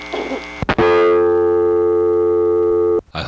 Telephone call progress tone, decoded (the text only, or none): dial tone